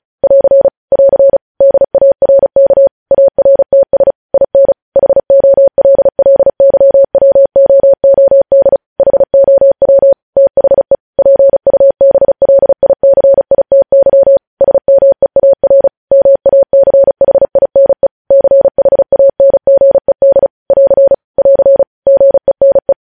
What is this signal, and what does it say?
Morse code, heard as a keyed tone, about 36 wpm, 550 Hz. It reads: <AR> <AR> DARK ARTS IN HOLLYWOOD HOW THE PUBLICITY SMEAR MACHINE CHANGED <AR> <AR> GENE
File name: podcast / News headlines-1x-35x35